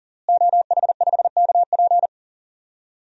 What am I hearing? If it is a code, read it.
Morse code, heard as a keyed tone, about 38 wpm, 700 Hz. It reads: OH5KP
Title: DXCC-100 Call Signs - 5 Characters - Repeated 2x - 40wpm